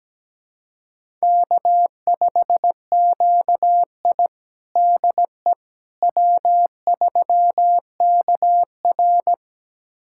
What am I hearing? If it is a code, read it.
Morse code, heard as a keyed tone, about 17 wpm, 700 Hz. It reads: K5QI DE W3KR
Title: USA Callsign DE Callsign - 4 Characters - 17wpm